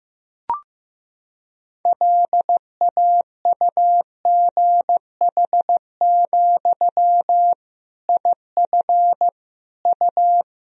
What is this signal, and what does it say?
Morse code, heard as a keyed tone, about 16 wpm, 700 Hz. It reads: LAUGH, IF U
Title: Sentences from Top 500 Words and Abbreviations 15wpm